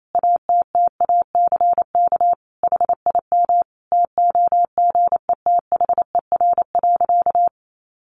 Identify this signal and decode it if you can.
Morse code, heard as a keyed tone, about 28 wpm, 700 Hz. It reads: ATTACK HIM TOGETHER.
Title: Sentences from Top 1500 Words 28wpm